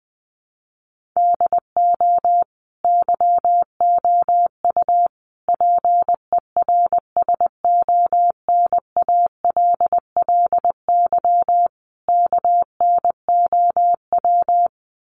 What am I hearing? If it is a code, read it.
Morse code, heard as a keyed tone, about 20 wpm, 700 Hz. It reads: DO YOU PERSONALLY KNOW
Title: Sentences from Top 1400 Words 20wpm